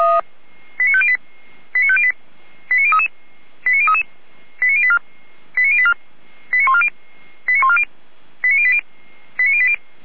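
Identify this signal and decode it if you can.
Five-tone selective call (ZVEI), heard as quick runs of five tones, tones of about 70 ms. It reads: 89508 89508 89033 89033 89084 89084 89140 89140 89080 89080
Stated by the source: Five-tone ZVEI selective call received by SDR, archived by feuerwehr-orlamuende (fire brigade Orlamünde).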